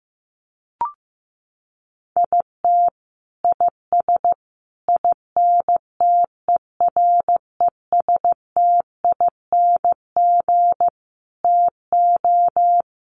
Morse code, heard as a keyed tone, about 15 wpm, 700 Hz. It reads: IT IS INTERESTING TO
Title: Sets of 5 Words Encore2 15wpm